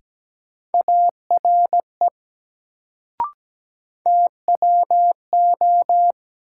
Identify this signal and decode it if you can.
Morse code, heard as a keyed tone, about 17 wpm, 700 Hz. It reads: ARE TWO
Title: Sets of 2 Words - New Word Drill - 17wpm